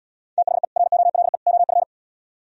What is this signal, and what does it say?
Morse code, heard as a keyed tone, about 76 wpm, 700 Hz. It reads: THE ROLE OF